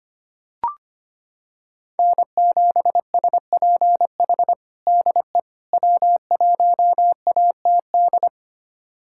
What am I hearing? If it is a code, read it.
Morse code, heard as a keyed tone, about 25 wpm, 700 Hz. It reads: N7SPH DE W1ATD
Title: USA Callsign DE Callsign - 5 Characters - 25wpm